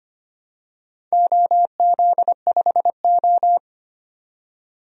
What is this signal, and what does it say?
Morse code, heard as a keyed tone, about 25 wpm, 700 Hz. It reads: OZ5O
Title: DXCC-100 Call Signs - 4 Characters - 25wpm